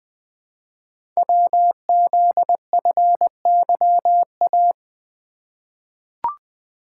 Morse code, heard as a keyed tone, about 20 wpm, 700 Hz. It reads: WZFYA
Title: Code Groups - 5 Letters - 20wpm